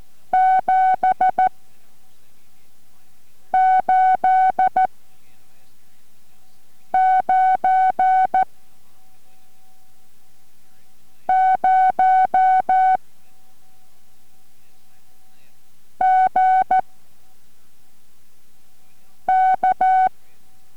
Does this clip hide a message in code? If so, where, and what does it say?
Yes: Morse code from 0.3 to 20.1 s, heard as a keyed tone, about 14 wpm, 750 Hz. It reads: 7 8 9 0 G K